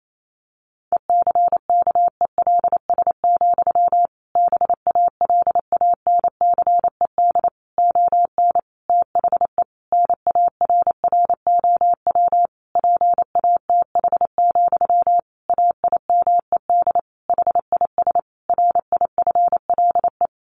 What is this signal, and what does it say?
Morse code, heard as a keyed tone, about 28 wpm, 700 Hz. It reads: ECKELS, BALANCED ON THE NARROW PATH, AIMED HIS RIFLE
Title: A Sound Of Thunder 28wpm